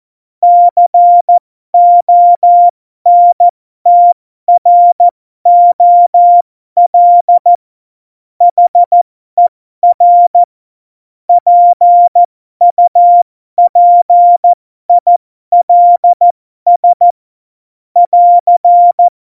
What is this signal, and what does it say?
Morse code, heard as a keyed tone, about 14 wpm, spaced as for 12 wpm, 700 Hz. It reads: CONTROL HER PUPILS <AR>
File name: Morsekurs / Witze